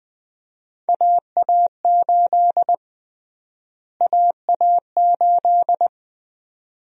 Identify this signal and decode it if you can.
Morse code, heard as a keyed tone, about 21 wpm, 700 Hz. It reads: AA8 AA8
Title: USA Call Sign Prefixes - Repeated 2x - 20wpm